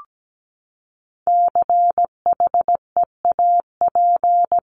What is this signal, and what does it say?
Morse code, heard as a keyed tone, about 17 wpm, 700 Hz. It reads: CHEAP
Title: Top 1001-1250 Words 17wpm